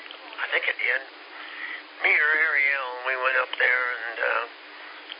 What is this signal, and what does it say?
Emergency Alert System (SAME) tones, heard as harsh buzzing bursts, with speech. It is heard from 0.3 to 4.3 s.